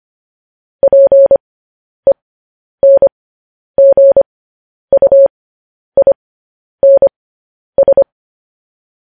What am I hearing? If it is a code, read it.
Morse code, heard as a keyed tone, about 25 wpm, spaced as for 10 wpm, 550 Hz. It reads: PENGUINS
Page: news headlines in morse code at 10 wpm